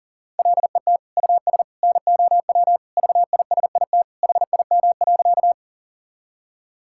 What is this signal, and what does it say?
Morse code, heard as a keyed tone, about 40 wpm, 700 Hz. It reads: LET US NOW VISIT HIM.
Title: Sentences from Top 1000 Words - Review Part 6 - 40wpm